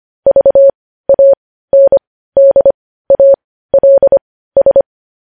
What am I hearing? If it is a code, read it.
Morse code, heard as a keyed tone, about 25 wpm, spaced as for 15 wpm, 550 Hz. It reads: VANDALS
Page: news-headlines-in-morse-code-at-15-wpm